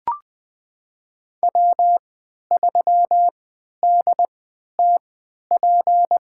Morse code, heard as a keyed tone, about 20 wpm, spaced as for 11 wpm, 700 Hz. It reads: W3DTP
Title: USA Call Signs - 5 Characters - 3x Character Spacing - 20wpm